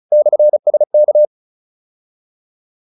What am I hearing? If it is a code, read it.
Morse code, heard as a keyed tone, about 34 wpm, 600 Hz. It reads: /SK
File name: QSO306 35